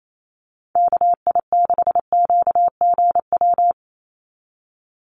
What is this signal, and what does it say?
Morse code, heard as a keyed tone, about 28 wpm, 700 Hz. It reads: KI6QGW
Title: USA Call Signs 28wpm